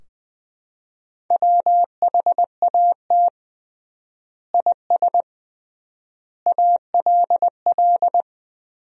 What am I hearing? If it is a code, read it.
Morse code, heard as a keyed tone, about 20 wpm, 700 Hz. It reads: WHAT IS ALL
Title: 3x Word-Spacing - Sentences from Top 100 Words - 20wpm